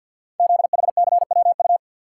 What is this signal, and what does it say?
Morse code, heard as a keyed tone, about 52 wpm, 700 Hz. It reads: ZHCWU
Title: Code Groups - 5 Letters - 50wpm